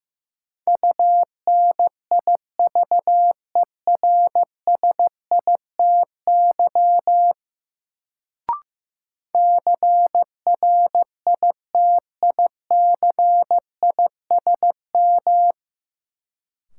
Morse code, heard as a keyed tone, about 15 wpm, 700 Hz. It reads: UNIVERSITY CRITICISM
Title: Sentences from Top 1300 Words - New Word Drill Alt - 15wpm